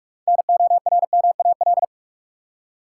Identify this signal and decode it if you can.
Morse code, heard as a keyed tone, about 46 wpm, 700 Hz. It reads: NORMAL
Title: Six Letter Words - 45wpm